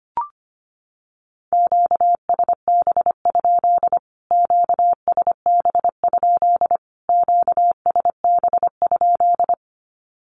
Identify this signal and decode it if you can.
Morse code, heard as a keyed tone, about 25 wpm, 700 Hz. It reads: QSB? QSB? QSB?
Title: Q-Codes Repeated 3x 25wpm